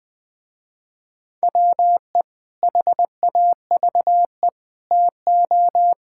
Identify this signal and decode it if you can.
Morse code, heard as a keyed tone, about 21 wpm, 700 Hz. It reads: WE HAVE TO
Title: Sentences with 3-Word Phrases - Spoken Beginning - 20wpm